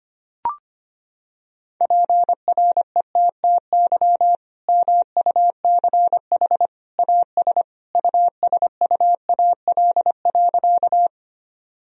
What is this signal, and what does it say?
Morse code, heard as a keyed tone, about 25 wpm, 700 Hz. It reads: PRETTY MUCH AS USUAL.
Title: Sentences from Top 1000 Words - Review Part 2 - 25wpm